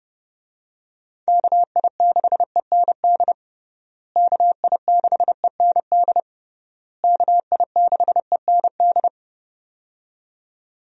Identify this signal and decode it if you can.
Morse code, heard as a keyed tone, about 30 wpm, 700 Hz. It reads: KI6END KI6END KI6END
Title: USA Call Signs - 6 Characters - Repeated 3x - 30wpm